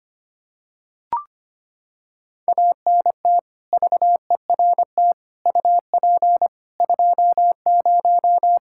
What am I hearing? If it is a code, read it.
Morse code, heard as a keyed tone, about 25 wpm, 700 Hz. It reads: ANT VERT UP 20
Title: K1USN SST QSO Practice 25wpm